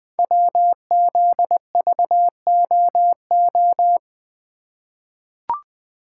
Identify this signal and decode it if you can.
Morse code, heard as a keyed tone, about 20 wpm, 700 Hz. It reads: WZVOO
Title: Code Groups - 5 Letter:Numbers - 20wpm